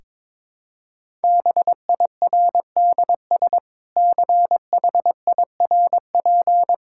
Morse code, heard as a keyed tone, about 22 wpm, 700 Hz. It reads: BIRDS CHIRP
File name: Difficult Sentences with Hints - 22wpm